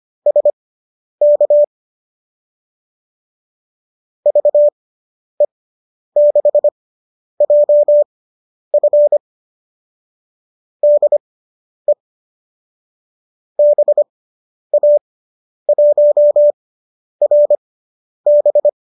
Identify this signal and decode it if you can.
Morse code, heard as a keyed tone, about 25 wpm, spaced as for 10 wpm, 590 Hz. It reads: SK VE6JF DE BA1RB